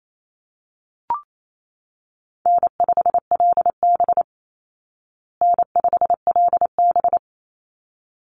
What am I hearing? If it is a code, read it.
Morse code, heard as a keyed tone, about 28 wpm, 700 Hz. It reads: N5LB N5LB
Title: Call Signs Repeated 3x - 28wpm